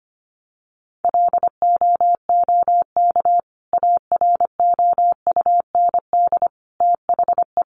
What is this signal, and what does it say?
Morse code, heard as a keyed tone, about 25 wpm, 700 Hz. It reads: LOOK AROUND THE
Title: Sentences from Top 600 Words 25wpm